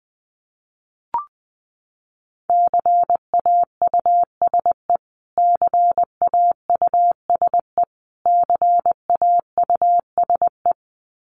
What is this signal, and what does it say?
Morse code, heard as a keyed tone, about 20 wpm, 700 Hz. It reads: CAUSE CAUSE CAUSE
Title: Top 300 Words Repeated 20wpm